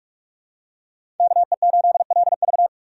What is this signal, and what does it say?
Morse code, heard as a keyed tone, about 47 wpm, 700 Hz. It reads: KE8LV
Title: USA Call Signs - 5 Characters - 6x Character Spacing - 45wpm